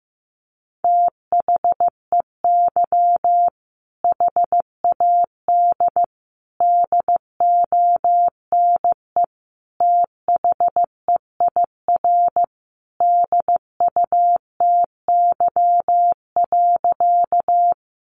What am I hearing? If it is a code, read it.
Morse code, heard as a keyed tone, about 15 wpm, 700 Hz. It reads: THEY HAD DONE THEIR DUTY.
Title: Sentences from Top 1500 Words 15wpm